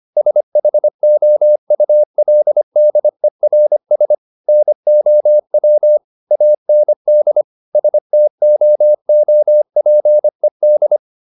Morse code, heard as a keyed tone, about 25 wpm, 600 Hz. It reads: SHOULDERS NOW AND STOOPED